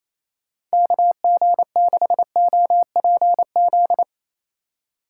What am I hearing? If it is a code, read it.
Morse code, heard as a keyed tone, about 34 wpm, 700 Hz. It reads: KG6OPZ
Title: Speed Racing - USA 2x3 Call Signs 28wpm